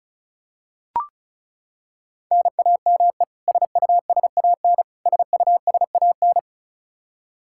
Morse code, heard as a keyed tone, about 36 wpm, 700 Hz. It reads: NAME SUSAN SUSAN
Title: Practice QSO - Exchange 1 - Beginning - 35wpm